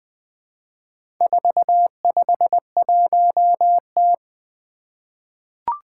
Morse code, heard as a keyed tone, about 20 wpm, 700 Hz. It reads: 451T
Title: Four Challenging Letter-Number Combinations 20wpm